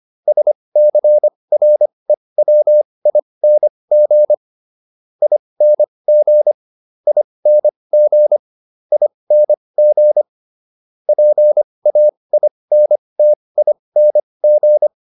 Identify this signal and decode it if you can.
Morse code, heard as a keyed tone, about 25 wpm, spaced as for 20 wpm, 600 Hz. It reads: SCREWING ING ING ING PAINTING